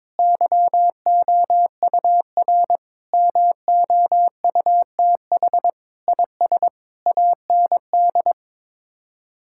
Morse code, heard as a keyed tone, about 22 wpm, 700 Hz. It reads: YOUR MOUTH IS AND
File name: practice / 22wpm / Sets of 8 Words - 22wpm